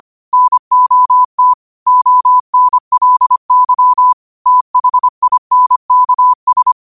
Morse code, heard as a keyed tone, about 25 wpm, 1000 Hz. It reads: NOT ONLY THINKS
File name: qotdfiles / 25wpm